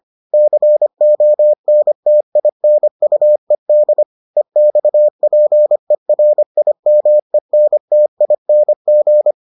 Morse code, heard as a keyed tone, about 25 wpm, 600 Hz. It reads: CONTINUED EXPERIMENTING